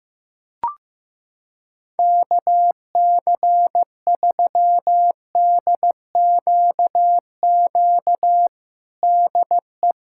Morse code, heard as a keyed tone, about 15 wpm, 700 Hz. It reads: KC3DQQ DE
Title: USA Callsign DE Callsign 15wpm